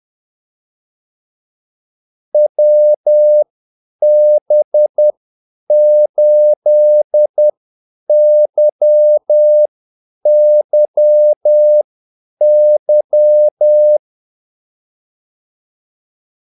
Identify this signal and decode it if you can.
Morse code, heard as a keyed tone, about 10 wpm, spaced as for 8 wpm, 600 Hz. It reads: WB8YYY
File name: Callsigns-10-8wpm